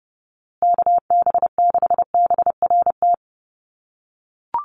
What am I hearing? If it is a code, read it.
Morse code, heard as a keyed tone, about 30 wpm, 700 Hz. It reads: KB6BRT
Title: USA Call Signs Repeated 3x 30wpm